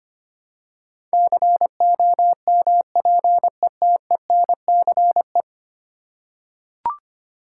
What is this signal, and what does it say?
Morse code, heard as a keyed tone, about 25 wpm, 700 Hz. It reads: COMPETENCE
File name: 10 Letter Words - First 3 Letters Given - 25wpm